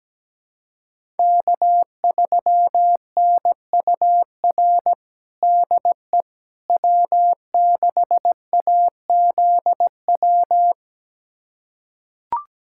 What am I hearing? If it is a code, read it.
Morse code, heard as a keyed tone, about 17 wpm, 700 Hz. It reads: K3NUR DE W6AZW
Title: USA Callsign DE Callsign - 5 Characters - 17wpm